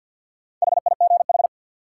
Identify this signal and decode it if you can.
Morse code, heard as a keyed tone, about 53 wpm, 700 Hz. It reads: HIGH
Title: Sentences from Top 200 Words - New Word Drill - 50wpm